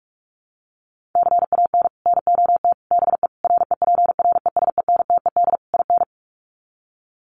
Morse code, heard as a keyed tone, about 44 wpm, 700 Hz. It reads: CAN NOT BE REPRESENTED IN
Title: Sets of 5 Words 45wpm